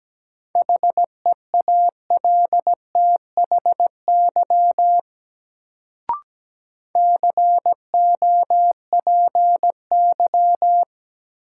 Morse code, heard as a keyed tone, about 17 wpm, 700 Hz. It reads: HEALTHY COPY
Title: Sentences from Top 1400 Words - New Word Drill Alt - 17wpm